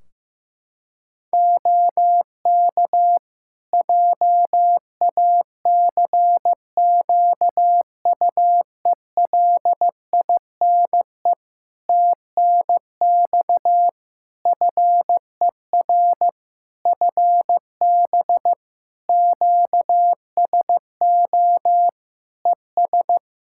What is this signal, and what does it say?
Morse code, heard as a keyed tone, about 15 wpm, 700 Hz. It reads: OK JACQUELINE TNX FER FB QSO ES
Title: K1USN SST QSO Practice 15wpm